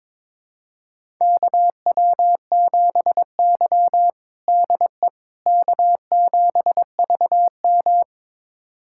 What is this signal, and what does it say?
Morse code, heard as a keyed tone, about 22 wpm, 700 Hz. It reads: KW7Y DE K7VM
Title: USA Callsign DE Callsign - 4 Characters - 22wpm